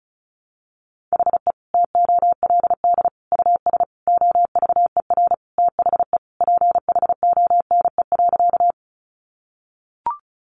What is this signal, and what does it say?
Morse code, heard as a keyed tone, about 34 wpm, 700 Hz. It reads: HE TOLD US OVER THE PHONE.
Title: Sentences from Top 1100 Words 35wpm